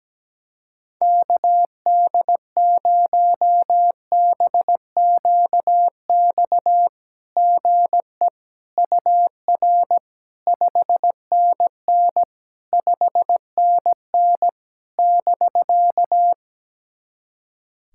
Morse code, heard as a keyed tone, about 17 wpm, 700 Hz. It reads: KD0BQX GE UR 5NN 5NN <BK>
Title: SOTA QSO Practice 17wpm